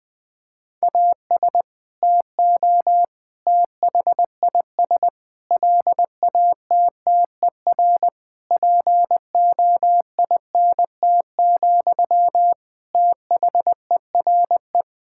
Morse code, heard as a keyed tone, about 19 wpm, 700 Hz. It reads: AS TO THIS LATTER POINT, THERE
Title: Sentences from Top 1600 Words 20wpm